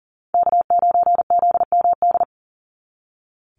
Morse code, heard as a keyed tone, about 40 wpm, 700 Hz. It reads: K9ZMD
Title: NA SOTA Chaser Callsigns 40wpm